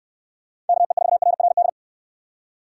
Morse code, heard as a keyed tone, about 65 wpm, 700 Hz. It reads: KE4RRD